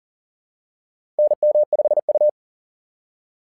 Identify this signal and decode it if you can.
Morse code, heard as a keyed tone, about 40 wpm, 600 Hz. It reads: NM5U
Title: SOTA QSO Practice 40wpm